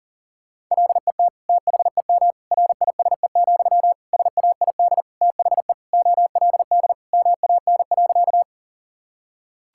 Morse code, heard as a keyed tone, about 37 wpm, 700 Hz. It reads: LET THEM RISE, SAID THE OLD MAN.
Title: Sentences from Top 1000 Words - Review Part 6 - 40wpm